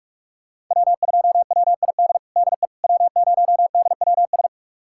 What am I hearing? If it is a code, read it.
Morse code, heard as a keyed tone, about 42 wpm, 700 Hz. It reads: W2WID DE W0DWS
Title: USA Callsign DE Callsign - 5 Characters - 45wpm